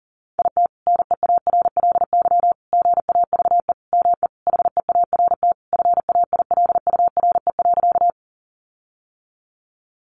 Morse code, heard as a keyed tone, about 40 wpm, 700 Hz. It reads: IT NEARLY GAVE ME HEART FAILURE.
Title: Sentences from Top 1300 Words 40wpm